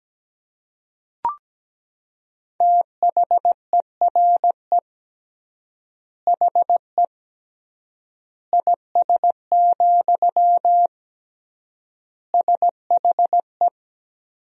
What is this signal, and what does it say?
Morse code, heard as a keyed tone, about 17 wpm, 700 Hz. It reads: THERE HE IS, SHE
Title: 3x Word-Spacing - Sentences from Top 100 Words - 17wpm